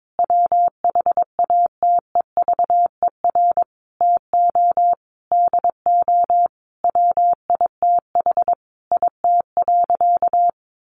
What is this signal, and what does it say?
Morse code, heard as a keyed tone, about 22 wpm, 700 Hz. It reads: WHATEVER TO DO WITH IT.
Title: Sentences from Top 1100 Words 22wpm